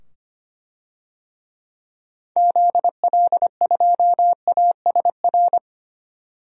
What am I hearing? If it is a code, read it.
Morse code, heard as a keyed tone, about 25 wpm, 700 Hz. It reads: ZL2ASR
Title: DXCC-100 Call Signs - 6 Characters - 25wpm